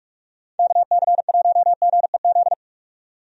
Morse code, heard as a keyed tone, about 43 wpm, 700 Hz. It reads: KC1GEZ
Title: USA Call Signs - 6 Characters - 45wpm